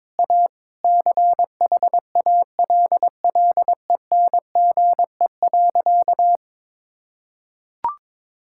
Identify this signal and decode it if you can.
Morse code, heard as a keyed tone, about 22 wpm, 700 Hz. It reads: A CHALLENGE.
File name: practice / 22wpm / Sentences with 4-Word Phrases - 22wpm